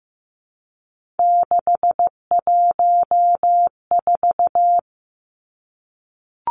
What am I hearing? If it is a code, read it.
Morse code, heard as a keyed tone, about 15 wpm, 700 Hz. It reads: 614
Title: Three Challenging Letter-Number Combinations 15wpm